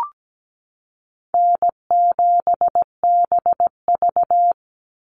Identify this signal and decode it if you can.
Morse code, heard as a keyed tone, about 17 wpm, 700 Hz. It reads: N7BV
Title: Easy Call Signs Repeated 2x - 17wpm